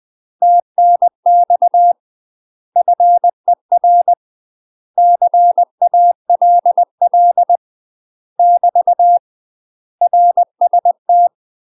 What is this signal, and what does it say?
Morse code, heard as a keyed tone, about 20 wpm, 700 Hz. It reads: TNX FER CALL <BT> RST
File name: UK MASTER QSOs TRAINING / 20WPM / UK MASTER QSOs 0038